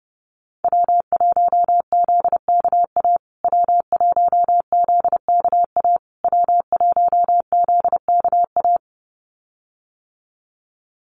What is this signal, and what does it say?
Morse code, heard as a keyed tone, about 30 wpm, 700 Hz. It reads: W1ZKA W1ZKA W1ZKA
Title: USA Call Signs Repeated 3x 30wpm